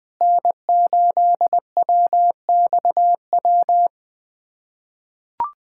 Morse code, heard as a keyed tone, about 20 wpm, 700 Hz. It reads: N8WXW
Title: USA Callsign DE Callsign - 5 Characters - 20wpm